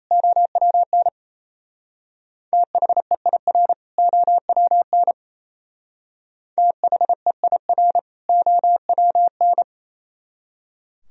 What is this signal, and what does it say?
Morse code, heard as a keyed tone, about 34 wpm, 690 Hz. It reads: OWN THEIR OWN THEIR OWN
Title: Speed-Racing - Sets of 2 Words 28wpm